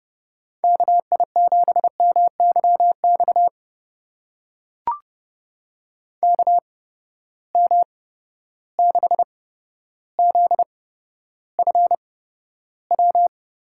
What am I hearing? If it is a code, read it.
Morse code, heard as a keyed tone, about 30 wpm, spaced as for 13 wpm, 700 Hz. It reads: KI7MYX KM6ZFW
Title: USA Call Signs - 6 Characters - 8x Character Spacing - 30wpm